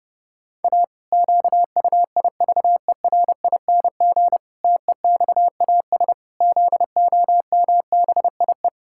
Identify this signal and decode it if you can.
Morse code, heard as a keyed tone, about 32 wpm, 700 Hz. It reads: A QUIVERING TEXAS ZOMBIE
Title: Pangram Challenge v2 - 30wpm